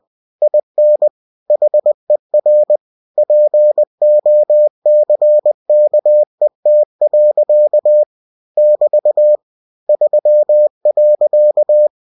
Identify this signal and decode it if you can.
Morse code, heard as a keyed tone, about 20 wpm, 600 Hz. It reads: IN HER POCKET. <BT> 3.
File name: cw-books / winnie-the-pooh / winnie-20wpm-page-0063